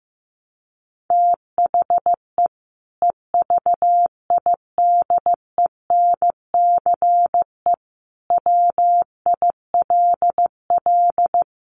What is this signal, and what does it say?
Morse code, heard as a keyed tone, about 15 wpm, 700 Hz. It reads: THE EVIDENCE WILL
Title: Sentences from Top 1300 Words 15wpm